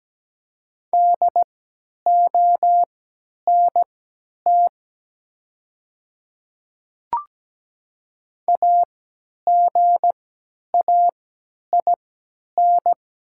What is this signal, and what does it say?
Morse code, heard as a keyed tone, about 17 wpm, spaced as for 10 wpm, 690 Hz. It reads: DONT AGAIN
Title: 3x Character Spacing - Top 300 Words - 17wpm